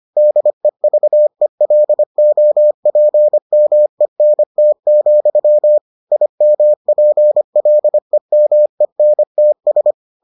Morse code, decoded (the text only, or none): DEVELOPMENT, IMPLEMENTS